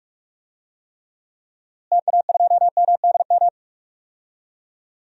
Morse code, heard as a keyed tone, about 44 wpm, 700 Hz. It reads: TA2MDM
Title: DXCC-100 Call Signs - 6 Characters - Repeated 2x - 45wpm